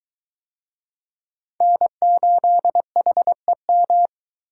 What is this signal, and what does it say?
Morse code, heard as a keyed tone, about 22 wpm, 700 Hz. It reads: N8HEM